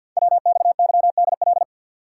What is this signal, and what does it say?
Morse code, heard as a keyed tone, about 46 wpm, 690 Hz. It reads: WXYDL
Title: Code Groups - 5 Letters - 50wpm